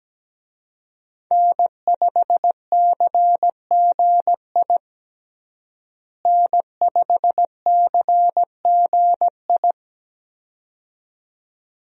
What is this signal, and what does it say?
Morse code, heard as a keyed tone, about 17 wpm, 700 Hz. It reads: N5CGI N5CGI
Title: USA Call Signs - 5 Characters - Repeated 2x - 17wpm